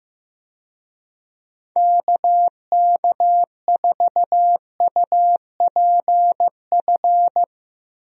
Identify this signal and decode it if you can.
Morse code, heard as a keyed tone, about 15 wpm, 700 Hz. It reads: KK4UPF
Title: USA Call Signs - 6 Characters - 15wpm